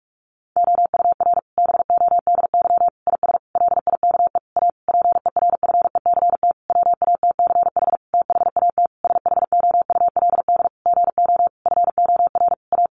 Morse code, heard as a keyed tone, about 44 wpm, 700 Hz. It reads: OUR BODY IS LIKE A PERFECT WATCH THAT SHOULD GO FOR A
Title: Sentences from Top 1000 Words 45wpm